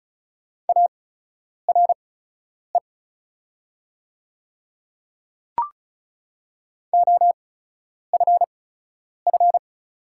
Morse code, heard as a keyed tone, about 36 wpm, spaced as for 10 wpm, 690 Hz. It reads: ARE OFF
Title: 8x Character Spacing - Top 300 Words - 35wpm